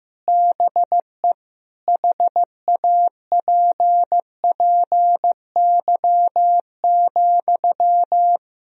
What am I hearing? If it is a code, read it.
Morse code, heard as a keyed tone, about 15 wpm, 700 Hz. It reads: BE HAPPY,
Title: Sentences from Top 1000 Words - Review Part 2 - 15wpm